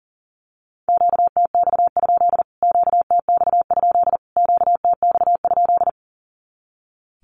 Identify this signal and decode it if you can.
Morse code, heard as a keyed tone, about 40 wpm, 700 Hz. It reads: QTX? QTX? QTX?
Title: Q-Codes Repeated 3x 40wpm